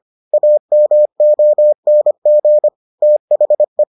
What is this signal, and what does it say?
Morse code, heard as a keyed tone, about 25 wpm, 600 Hz. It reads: AMONG THE